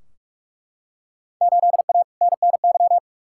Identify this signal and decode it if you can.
Morse code, heard as a keyed tone, about 42 wpm, 700 Hz. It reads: 8A NNY
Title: Field Day Exchanges Encore - 45wpm